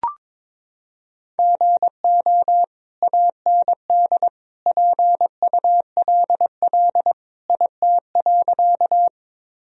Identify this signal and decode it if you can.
Morse code, heard as a keyed tone, about 22 wpm, 700 Hz. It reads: GO AND PULL IT.
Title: Sentences from Top 1000 Words - Review Part 2 - 22wpm